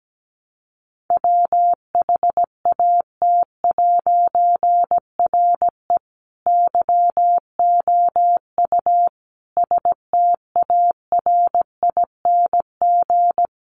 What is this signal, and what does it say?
Morse code, heard as a keyed tone, about 17 wpm, 700 Hz. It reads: WHAT'RE YOU STARING
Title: A Sound Of Thunder 17wpm